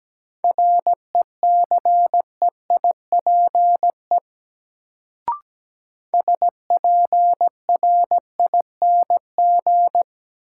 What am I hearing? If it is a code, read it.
Morse code, heard as a keyed tone, about 17 wpm, 700 Hz. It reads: RECEIPE SPRING
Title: Sentences from Top 1200 Words - New Word Drill - 17wpm